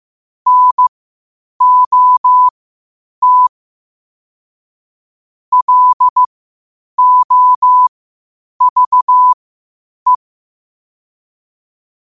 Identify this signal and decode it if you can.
Morse code, heard as a keyed tone, about 15 wpm, spaced as for 8 wpm, 1000 Hz. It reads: NOT LOVE
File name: 8wpm